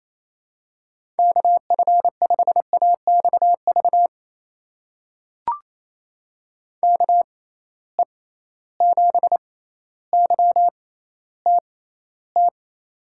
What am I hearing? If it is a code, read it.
Morse code, heard as a keyed tone, about 27 wpm, spaced as for 14 wpm, 700 Hz. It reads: KF5AXV KE7YTT
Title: USA Call Signs - 6 Characters - 6x Character Spacing - 28wpm